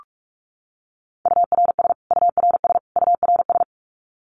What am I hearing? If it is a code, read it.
Morse code, heard as a keyed tone, about 44 wpm, 700 Hz. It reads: URS URS URS
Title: Abbreviations Repeated 3x - 45wpm